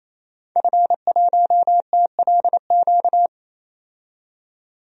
Morse code, heard as a keyed tone, about 28 wpm, 700 Hz. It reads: F1TLQ
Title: DXCC-100 Call Signs - 5 Characters - 28wpm